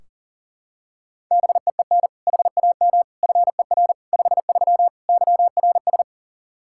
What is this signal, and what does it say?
Morse code, heard as a keyed tone, about 40 wpm, 700 Hz. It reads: BEEN HAM FER 53 YRS
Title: Practice QSO - Exchange 3 - More Info - 40wpm